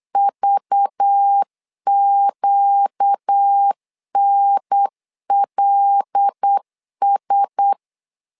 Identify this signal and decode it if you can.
Morse code, heard as a keyed tone, about 8 wpm, 800 Hz. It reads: VQNLS
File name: LEZIONI / 5 LBVC / 5 LBVC 8wpm 2935